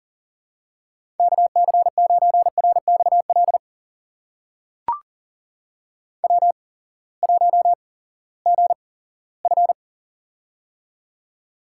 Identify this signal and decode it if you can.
Morse code, heard as a keyed tone, about 37 wpm, spaced as for 20 wpm, 700 Hz. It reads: KC9RXL W1GF
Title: USA Call Signs - 8x Character Spacing - 40wpm